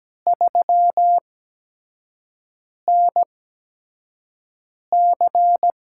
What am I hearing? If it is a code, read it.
Morse code, heard as a keyed tone, about 17 wpm, spaced as for 5 wpm, 700 Hz. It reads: 3NC
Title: USA Call Signs - 5 Characters - 8x Character Spacing - 17wpm